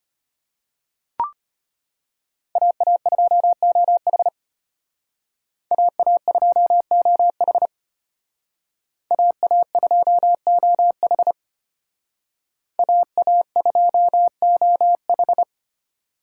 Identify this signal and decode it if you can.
Morse code, heard as a keyed tone, about 32 wpm, 700 Hz. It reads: AA2OH AA2OH AA2OH AA2OH